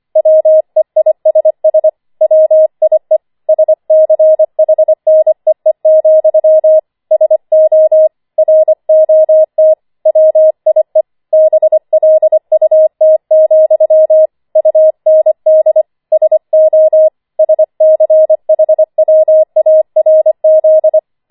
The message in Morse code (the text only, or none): WEISS WIE SCHNEE, SO ROT WIE BLUT, UND SO SCHWARZ